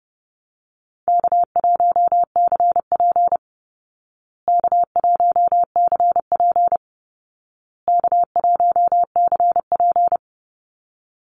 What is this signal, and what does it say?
Morse code, heard as a keyed tone, about 30 wpm, 700 Hz. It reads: K1CP K1CP K1CP
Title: Easy Call Signs Repeated 3x - 30wpm